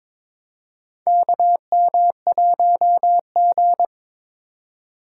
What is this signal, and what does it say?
Morse code, heard as a keyed tone, about 22 wpm, 700 Hz. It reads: KM1G
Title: USA Call Signs - 4 Characters - 4x Character Spacing - 22wpm